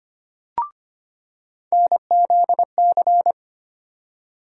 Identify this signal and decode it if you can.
Morse code, heard as a keyed tone, about 25 wpm, 700 Hz. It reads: NZC
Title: USA Call Sign Postfixes - 3 Letters 25wpm